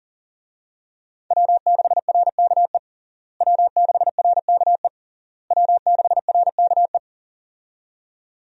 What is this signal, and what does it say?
Morse code, heard as a keyed tone, about 37 wpm, 700 Hz. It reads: W6RKE W6RKE W6RKE
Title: USA Call Signs - 5 Characters - Repeated 3x - 40wpm